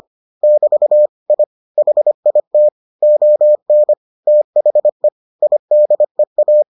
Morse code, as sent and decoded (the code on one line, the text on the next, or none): -...- / .. / .... .. - / --- -. / - .... / .. -.. .-
<BT> I HIT ON THE IDEA